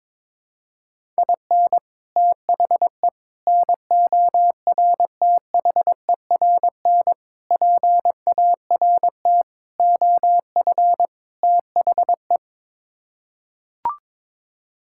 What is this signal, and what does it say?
Morse code, heard as a keyed tone, about 22 wpm, 700 Hz. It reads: IN THE NORTHERN PART OF THE
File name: practice / 22wpm / Sets of 6 Words - 22wpm